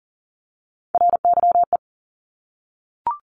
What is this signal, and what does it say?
Morse code, heard as a keyed tone, about 40 wpm, 700 Hz. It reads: RYE
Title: Multi-letter Phonemes 40wpm